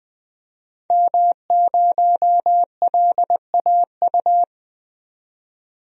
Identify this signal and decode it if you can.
Morse code, heard as a keyed tone, about 20 wpm, 700 Hz. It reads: M0LAU
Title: DXCC-100 Call Signs - 5 Characters - 20wpm